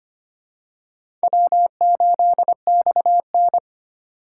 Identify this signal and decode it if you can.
Morse code, heard as a keyed tone, about 25 wpm, 700 Hz. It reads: W8XN
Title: USA Call Signs - 4 Characters - 4x Character Spacing - 25wpm